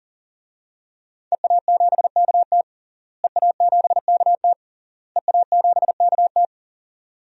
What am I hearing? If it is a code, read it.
Morse code, heard as a keyed tone, about 40 wpm, 700 Hz. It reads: EA7KT EA7KT EA7KT
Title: DXCC-100 Call Signs - 5 Characters - Repeated 3x - 40wpm